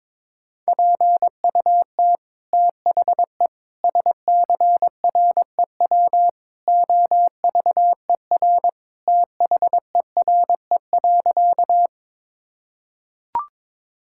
Morse code, heard as a keyed tone, about 22 wpm, 700 Hz. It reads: PUT THE SCREW OVER THERE.
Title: Sentences from Top 1600 Words 22wpm